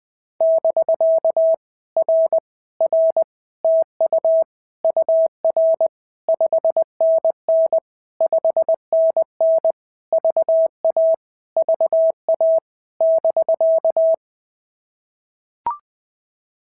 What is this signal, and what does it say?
Morse code, heard as a keyed tone, about 20 wpm, 650 Hz. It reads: <BK> R R TU UR 5NN 5NN VA VA <BK>
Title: POTA QSO Practice - 20wpm